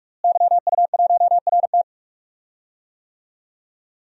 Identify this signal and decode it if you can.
Morse code, heard as a keyed tone, about 46 wpm, 700 Hz. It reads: YU1RT
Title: DXCC-100 Call Signs - 5 Characters - Repeated 3x - 45wpm